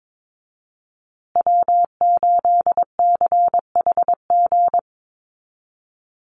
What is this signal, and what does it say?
Morse code, heard as a keyed tone, about 22 wpm, 700 Hz. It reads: W8CHG
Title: USA Call Signs Repeated 3x 22wpm